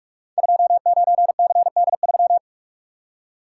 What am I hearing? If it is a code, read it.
Morse code, heard as a keyed tone, about 44 wpm, 700 Hz. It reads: 29CD3
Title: Code Groups - 5 Letter:Numbers - 45wpm